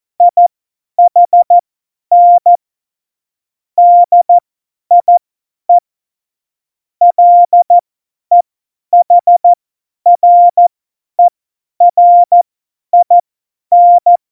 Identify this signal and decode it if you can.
Morse code, heard as a keyed tone, about 14 wpm, spaced as for 10 wpm, 700 Hz. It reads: IHN DIE LEHRERIN
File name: Morsekurs / Witze / Witz 19. Januar 2021, 50 F70